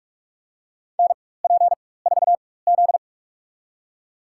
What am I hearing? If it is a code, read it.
Morse code, heard as a keyed tone, about 42 wpm, spaced as for 20 wpm, 700 Hz. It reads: NP4Z
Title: USA Call Signs - 4 Characters - 4x Character Spacing - 45wpm